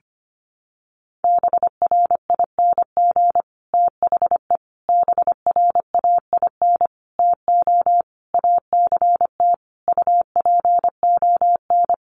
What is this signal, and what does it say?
Morse code, heard as a keyed tone, about 25 wpm, 700 Hz. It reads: BRING THE BRAIN TO ACT UPON
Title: Sentences from Top 1200 Words 25wpm